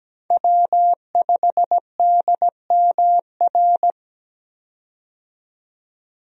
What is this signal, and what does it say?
Morse code, heard as a keyed tone, about 17 wpm, 700 Hz. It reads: W5DMR
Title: USA Call Signs - 5 Characters - 17wpm